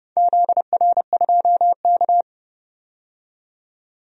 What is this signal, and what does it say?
Morse code, heard as a keyed tone, about 30 wpm, 700 Hz. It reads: ZR2K
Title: DXCC-100 Call Signs - 4 Characters - 30wpm